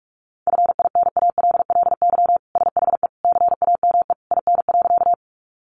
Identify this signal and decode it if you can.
Morse code, heard as a keyed tone, about 44 wpm, 700 Hz. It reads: FINALLY SHE CAME IN.
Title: Sentences from Top 1000 Words 45wpm